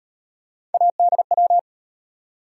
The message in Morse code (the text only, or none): ADW